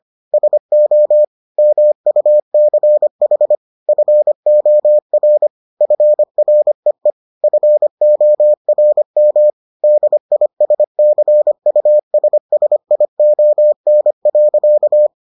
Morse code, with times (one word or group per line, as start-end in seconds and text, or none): 0.3-1.2 SO
1.6-3.5 MUCH
3.9-5.5 FOR
5.8-7.1 FREE
7.4-9.5 FORM
9.8-15.1 DISCUSSION.